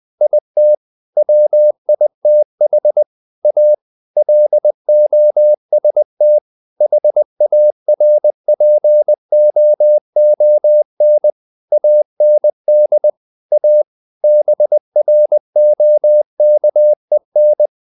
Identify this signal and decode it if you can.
Morse code, heard as a keyed tone, about 20 wpm, 600 Hz. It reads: IT WITH A LOST HARPOON AND A BROKEN
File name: the old man and the sea part 2 20wpm